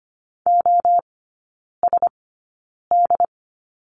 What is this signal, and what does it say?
Morse code, heard as a keyed tone, about 25 wpm, spaced as for 9 wpm, 690 Hz. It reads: OSD
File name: Lesson 15e - Number 5 - Three Characters 25wpm